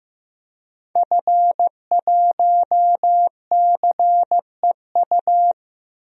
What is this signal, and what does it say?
Morse code, heard as a keyed tone, about 15 wpm, 700 Hz. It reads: F1CEU
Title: DXCC-100 Call Signs - 5 Characters - 15wpm